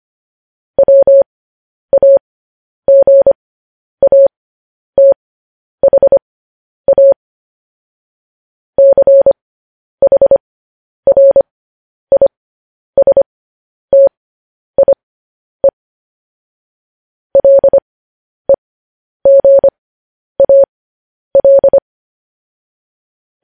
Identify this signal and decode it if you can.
Morse code, heard as a keyed tone, about 25 wpm, spaced as for 10 wpm, 550 Hz. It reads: WAGATHA CHRISTIE LEGAL
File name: podcast / News headlines-1x-25x10